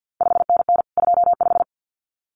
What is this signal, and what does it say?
Morse code, heard as a keyed tone, about 50 wpm, 700 Hz. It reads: 5NN 25
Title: CQ WW DX Contest Exchange - 50wpm